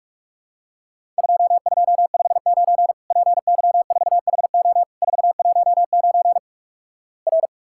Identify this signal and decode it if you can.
Morse code, heard as a keyed tone, about 47 wpm, 700 Hz. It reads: 2259 PY4HO 419 R
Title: Worked All Europe DX Contest - QTC Exchange - 45wpm